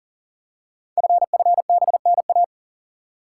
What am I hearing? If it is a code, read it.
Morse code, heard as a keyed tone, about 41 wpm, 690 Hz. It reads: FFBNA
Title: Code Groups - 5 Letters - 40wpm